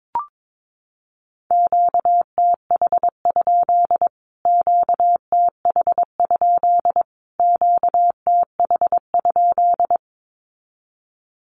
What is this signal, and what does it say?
Morse code, heard as a keyed tone, about 22 wpm, 700 Hz. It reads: QTH? QTH? QTH?
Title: Q-Codes Repeated 3x 22wpm